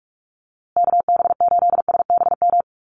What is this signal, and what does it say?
Morse code, heard as a keyed tone, about 44 wpm, 700 Hz. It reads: KB8SBM
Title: USA Call Signs Repeated 2x 45wpm